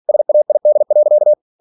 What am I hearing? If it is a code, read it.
Morse code, heard as a keyed tone, about 46 wpm, 600 Hz. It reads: SAID.
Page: Never 47wpm